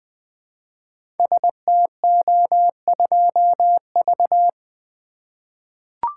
Morse code, heard as a keyed tone, about 20 wpm, 700 Hz. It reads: STO2V
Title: Code Groups - 5 Letter:Numbers - 20wpm